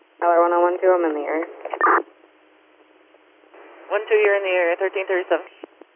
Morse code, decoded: none

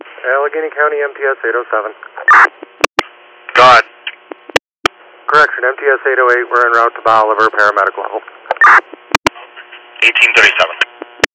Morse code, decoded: none

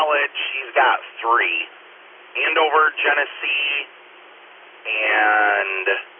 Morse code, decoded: none

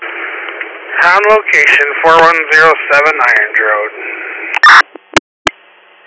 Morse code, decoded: none

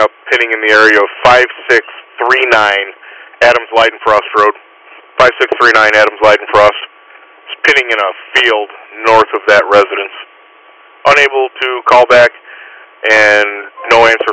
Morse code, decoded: none

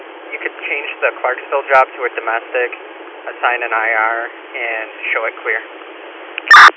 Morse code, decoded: none

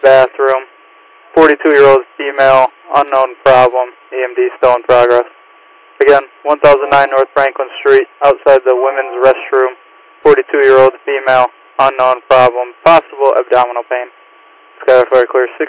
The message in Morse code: none

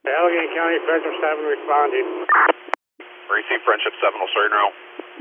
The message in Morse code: none

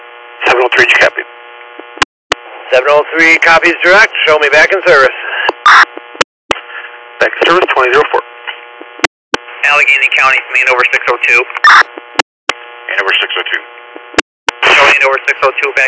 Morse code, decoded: none

